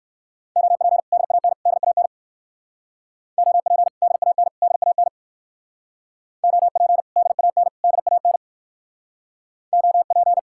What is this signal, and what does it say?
Morse code, heard as a keyed tone, about 56 wpm, 700 Hz. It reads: OP DAN DAN OP DAN DAN OP DAN DAN OP